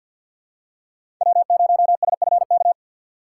Morse code, heard as a keyed tone, about 47 wpm, 700 Hz. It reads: W0SFK
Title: USA Call Signs - 6x Character Spacing - 50wpm